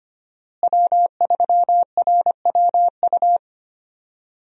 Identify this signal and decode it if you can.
Morse code, heard as a keyed tone, about 25 wpm, 700 Hz. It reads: W3RWU
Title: USA Call Signs - 5 Characters - Repeated 3x - 25wpm